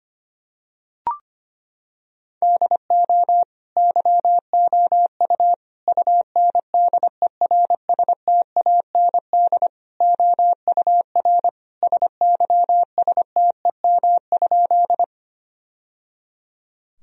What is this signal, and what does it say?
Morse code, heard as a keyed tone, about 25 wpm, 700 Hz. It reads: DO YOU UNDERSTAND OUR SYSTEM?
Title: Sentences from Top 1000 Words - Review Part 3 - 25wpm